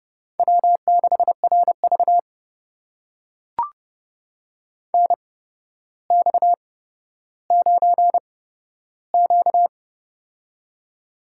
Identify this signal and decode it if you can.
Morse code, heard as a keyed tone, about 32 wpm, spaced as for 13 wpm, 700 Hz. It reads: W6RV NX9Q
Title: USA Call Signs - 4 Characters - 8x Character Spacing - 30wpm